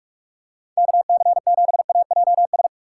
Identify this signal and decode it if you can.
Morse code, heard as a keyed tone, about 43 wpm, 700 Hz. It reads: KC7AJS
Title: USA Call Signs - 6 Characters - 2x Character Spacing - 45wpm